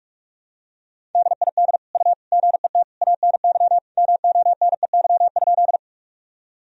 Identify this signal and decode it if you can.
Morse code, heard as a keyed tone, about 47 wpm, 700 Hz. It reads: DID U GET ANY MONEY?
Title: Sentences from Top 400 Words and Abbreviations 45wpm